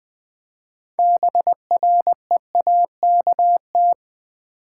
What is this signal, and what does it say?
Morse code, heard as a keyed tone, about 20 wpm, 700 Hz. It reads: BREAKT
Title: Build A Word - 6 Letters At A Time - 12-Letter Words - 20wpm